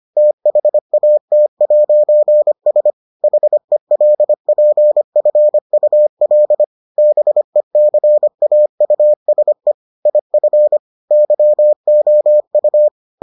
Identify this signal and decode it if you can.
Morse code, heard as a keyed tone, about 25 wpm, 600 Hz. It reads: THAT'S HELPFUL BECAUSE IF YOU